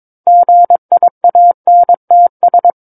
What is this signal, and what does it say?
Morse code, heard as a keyed tone, about 23 wpm, 700 Hz. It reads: GIANTS
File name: Book One / 13-Chapter 13 - 22 WPM